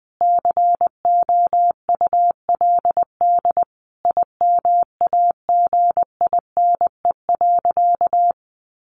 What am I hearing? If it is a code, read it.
Morse code, heard as a keyed tone, about 20 wpm, 700 Hz. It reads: COULD IMAGINE.
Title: Sentences from Top 1400 Words 20wpm